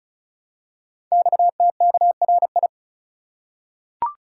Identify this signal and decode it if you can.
Morse code, heard as a keyed tone, about 36 wpm, 700 Hz. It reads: XTKRI
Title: Code Groups - 5 Letters - 35wpm